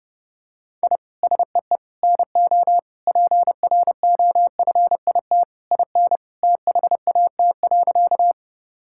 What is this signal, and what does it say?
Morse code, heard as a keyed tone, about 30 wpm, 700 Hz. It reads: I SEE NO PROFIT IN THAT.
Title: Sentences from Top 1000 Words - Review Part 5 - 30wpm